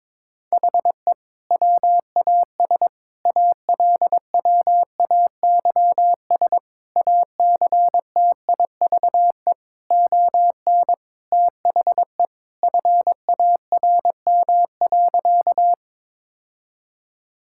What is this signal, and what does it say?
Morse code, heard as a keyed tone, about 22 wpm, 700 Hz. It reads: HE WAS ALWAYS ACTIVE ON THE FARM.
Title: Sentences from Top 1000 Words - Review Part 6 - 22wpm